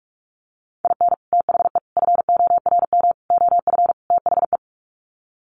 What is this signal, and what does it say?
Morse code, heard as a keyed tone, about 44 wpm, 700 Hz. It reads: IN THE FORM OF THE
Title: Sets of 5 Words Encore2 45wpm